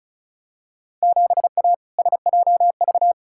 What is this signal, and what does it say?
Morse code, heard as a keyed tone, about 36 wpm, 690 Hz. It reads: 7A SJV
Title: Field Day Exchanges Encore - 35wpm